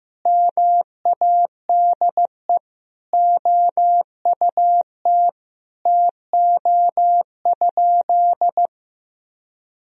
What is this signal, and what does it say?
Morse code, heard as a keyed tone, about 15 wpm, 700 Hz. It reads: MADE OUT TO?
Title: Sentences from Top 1000 Words - Review Part 3 - 15wpm